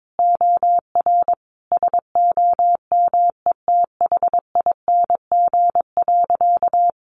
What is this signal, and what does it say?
Morse code, heard as a keyed tone, about 22 wpm, 700 Hz. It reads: OR SOMETHING.
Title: Sentences from Top 1000 Words 22wpm